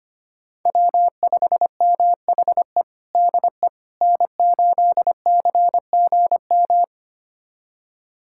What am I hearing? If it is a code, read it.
Morse code, heard as a keyed tone, about 25 wpm, 700 Hz. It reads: W5MHE DE N8CGM
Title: USA Callsign DE Callsign - 5 Characters - 25wpm